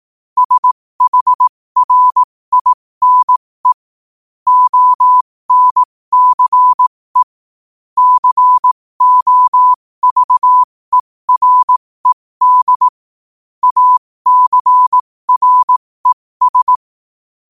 Morse code, heard as a keyed tone, about 18 wpm, spaced as for 15 wpm, 1000 Hz. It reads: SHRINE ONCE COVERED ACRES
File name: qotdfiles / 15wpm